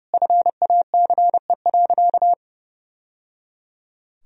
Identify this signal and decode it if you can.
Morse code, heard as a keyed tone, about 30 wpm, 700 Hz. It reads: FACE.
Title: Sentences from Top 1000 Words - Review Part 2 - 30wpm